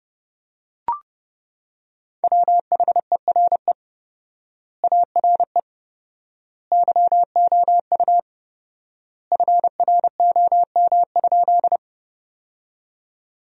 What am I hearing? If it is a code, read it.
Morse code, heard as a keyed tone, about 32 wpm, 700 Hz. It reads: WHERE ARE YOU FROM?
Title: 4x Word-Spacing - Sentences from Top 200 Words - 30wpm